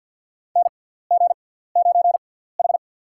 Morse code, heard as a keyed tone, about 47 wpm, spaced as for 17 wpm, 700 Hz. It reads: NG9H
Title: USA Call Signs - 4 Characters - 6x Character Spacing - 50wpm